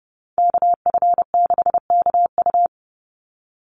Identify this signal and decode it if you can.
Morse code, heard as a keyed tone, about 30 wpm, 700 Hz. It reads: KF6KU
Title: USA Call Signs 30wpm